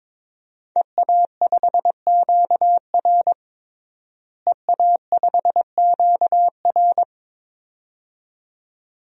Morse code, heard as a keyed tone, about 22 wpm, 700 Hz. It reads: EA5QR EA5QR
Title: DXCC-100 Call Signs - 5 Characters - Repeated 2x - 22wpm